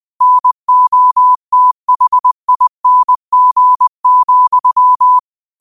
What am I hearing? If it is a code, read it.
Morse code, heard as a keyed tone, about 20 wpm, 1000 Hz. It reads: NOTHING,
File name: qotdfiles / 20wpm